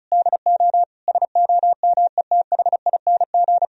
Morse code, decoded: DO SOMETHING

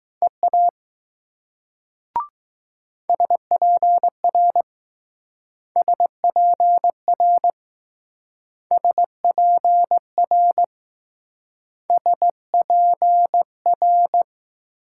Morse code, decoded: EA SPR SPR SPR S P R

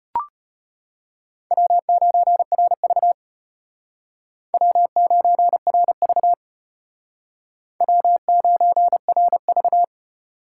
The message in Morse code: W9RV W9RV W9RV

